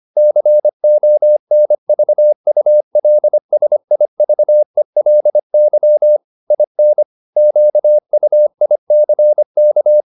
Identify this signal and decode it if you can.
Morse code, heard as a keyed tone, about 25 wpm, 600 Hz. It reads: CONVULSIVELY IN QUICK